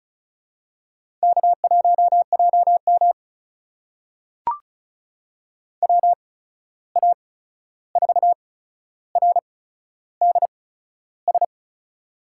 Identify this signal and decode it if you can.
Morse code, heard as a keyed tone, about 36 wpm, spaced as for 13 wpm, 700 Hz. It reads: K1JM WA4RDS
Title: USA Call Signs - 8x Character Spacing - 35wpm